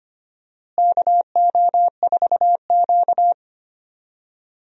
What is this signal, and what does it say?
Morse code, heard as a keyed tone, about 25 wpm, 700 Hz. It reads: KO4Q
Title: USA Call Signs - 4 Characters - Repeated 3x - 25wpm